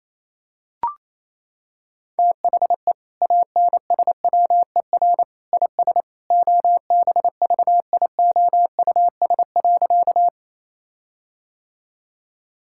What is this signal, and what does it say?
Morse code, heard as a keyed tone, about 28 wpm, 700 Hz. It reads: THE ANSWER IS OBVIOUS.
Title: Sentences from Top 1600 Words 28wpm